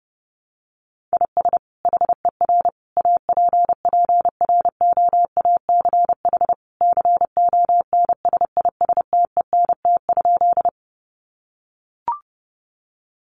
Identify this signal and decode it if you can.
Morse code, heard as a keyed tone, about 30 wpm, 700 Hz. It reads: IS HER APPROACH CONSISTENT?
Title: Sentences from Top 1500 Words 30wpm